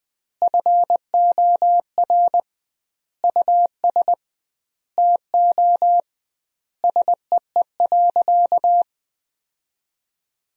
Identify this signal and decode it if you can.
Morse code, heard as a keyed tone, about 20 wpm, 700 Hz. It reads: FOR US TO SEE.
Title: 2x Word-Spacing - Sentences from Top 500 Words - 20wpm